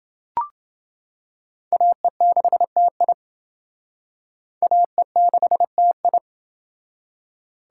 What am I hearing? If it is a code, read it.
Morse code, heard as a keyed tone, about 29 wpm, 700 Hz. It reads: AE6TI AE6TI